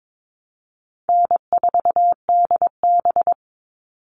Morse code, heard as a keyed tone, about 22 wpm, 700 Hz. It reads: N4DB
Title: All Cut Numbers - 4-Digit 22wpm